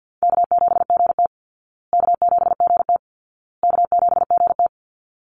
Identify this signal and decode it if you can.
Morse code, heard as a keyed tone, about 50 wpm, 700 Hz. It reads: K7GT K7GT K7GT
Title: Easy Call Signs Repeated 3x - 50wpm